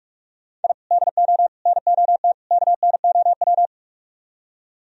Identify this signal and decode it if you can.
Morse code, heard as a keyed tone, about 44 wpm, 700 Hz. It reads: I DO NOT KNOW